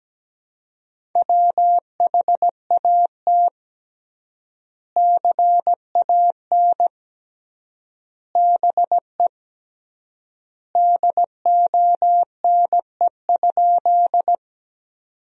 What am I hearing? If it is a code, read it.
Morse code, heard as a keyed tone, about 17 wpm, 700 Hz. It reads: WHAT CAN BE DONE?
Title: 3x Word-Spacing - Sentences from Top 500 Words - 17wpm